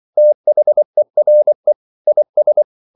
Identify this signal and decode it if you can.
Morse code, heard as a keyed tone, about 24 wpm, 600 Hz. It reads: THERE IS
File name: CW / Practice / EnglishQuotations0424